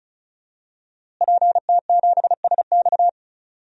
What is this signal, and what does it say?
Morse code, heard as a keyed tone, about 36 wpm, 700 Hz. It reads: PT7SX
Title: DXCC-100 Call Signs - 5 Characters - Repeated 2x - 35wpm